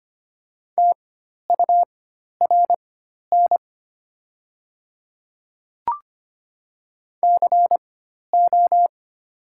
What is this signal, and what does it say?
Morse code, heard as a keyed tone, about 25 wpm, spaced as for 12 wpm, 700 Hz. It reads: TURN CO
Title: 4x Character Spacing - Top 200 Words - 25wpm